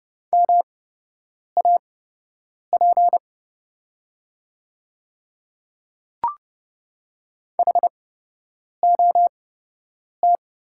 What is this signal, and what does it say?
Morse code, heard as a keyed tone, about 31 wpm, spaced as for 8 wpm, 700 Hz. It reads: MAP HOT
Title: 8x Character Spacing - Top 500 Words - 30wpm